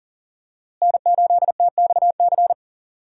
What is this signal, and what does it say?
Morse code, heard as a keyed tone, about 39 wpm, 700 Hz. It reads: N8TXC
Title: USA Call Signs - 5 Characters - Repeated 2x - 40wpm